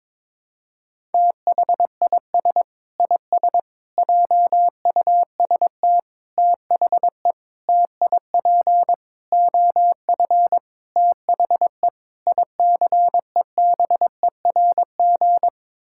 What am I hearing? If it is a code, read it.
Morse code, heard as a keyed tone, about 22 wpm, 700 Hz. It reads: THIS IS JUST THE TIP OF THE ICEBERG
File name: practice / 22wpm / Sets of 8 Words - 22wpm